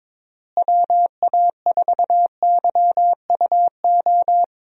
Morse code, heard as a keyed tone, about 22 wpm, 700 Hz. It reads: WA4YUO
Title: USA Call Signs - 6 Characters - Repeated 2x - 22wpm